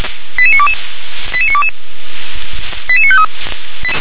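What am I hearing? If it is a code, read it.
Five-tone selective call (ZVEI), heard as quick runs of five tones, tones of about 70 ms. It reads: 88922 88922 88953
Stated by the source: ZVEI five-tone call received by SDR.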